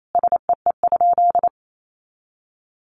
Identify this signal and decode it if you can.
Morse code, heard as a keyed tone, about 28 wpm, 700 Hz. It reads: SEE?
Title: Sentences from Top 100 Words 28wpm